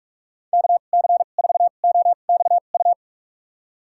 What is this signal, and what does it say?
Morse code, heard as a keyed tone, about 45 wpm, spaced as for 32 wpm, 700 Hz. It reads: KC4OXU
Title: USA Call Signs - 6 Characters - 2x Character Spacing - 45wpm